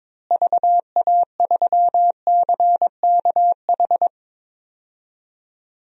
Morse code, heard as a keyed tone, about 22 wpm, 700 Hz. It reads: VA3CKH
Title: DXCC-100 Call Signs - 6 Characters - Repeated 3x - 22wpm